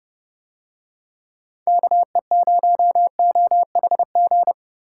Morse code, heard as a keyed tone, about 30 wpm, 700 Hz. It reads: KE0OHG